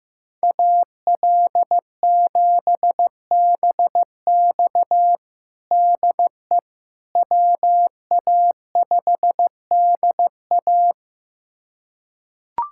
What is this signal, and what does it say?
Morse code, heard as a keyed tone, about 15 wpm, 700 Hz. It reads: AL7BX DE WA5DA
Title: USA Callsign DE Callsign - 5 Characters - 15wpm